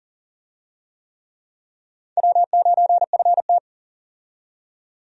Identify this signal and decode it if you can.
Morse code, heard as a keyed tone, about 40 wpm, 700 Hz. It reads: W9FT
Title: USA Call Signs - 4 Characters - 40wpm